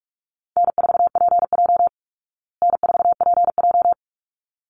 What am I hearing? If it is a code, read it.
Morse code, heard as a keyed tone, about 44 wpm, 700 Hz. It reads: N4PJ N4PJ
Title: Easy Call Signs Repeated 2x - 45wpm